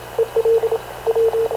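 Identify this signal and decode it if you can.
Morse code, heard as a keyed tone, about 28 wpm, 450 Hz, with radio noise. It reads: EL W
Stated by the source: Havana Radio / CLA sending Spanish press (PX) in Morse, recorded by Radio Officer Paul Dumesnil onboard CNAV QUEST/CZDO.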